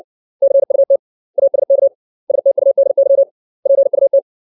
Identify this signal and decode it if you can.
Morse code, heard as a keyed tone, about 60 wpm, 550 Hz. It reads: CUT ASK STUDY PUT
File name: 632 words / 632 Words 60WPM